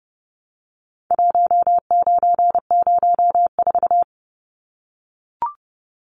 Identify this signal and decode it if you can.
Morse code, heard as a keyed tone, about 30 wpm, 700 Hz. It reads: 1904
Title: Numbers - 4-Digit 30wpm